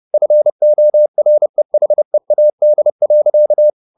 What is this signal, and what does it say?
Morse code, heard as a keyed tone, about 30 wpm, 600 Hz. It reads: FOREHEAD.